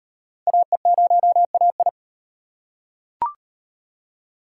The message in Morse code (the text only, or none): AE0AI